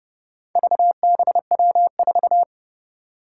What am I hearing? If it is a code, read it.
Morse code, heard as a keyed tone, about 30 wpm, 700 Hz. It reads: VBW4
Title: Four Challenging Letter-Number Combinations 30wpm